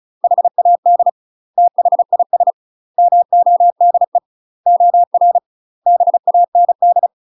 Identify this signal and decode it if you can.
Morse code, heard as a keyed tone, about 36 wpm, 700 Hz. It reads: HAD THIS MODE OR BAND
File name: UK MASTER QSOs TRAINING / 35WPM / UK MASTER QSOs 0016